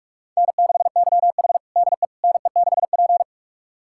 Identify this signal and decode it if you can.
Morse code, heard as a keyed tone, about 46 wpm, 700 Hz. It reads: N6YH DE NE6P
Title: USA Callsign DE Callsign - 4 Characters - 45wpm